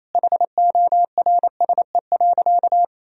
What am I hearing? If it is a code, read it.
Morse code, heard as a keyed tone, about 28 wpm, 700 Hz. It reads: HORSE.
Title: Sentences from Top 1000 Words - Review Part 2 - 28wpm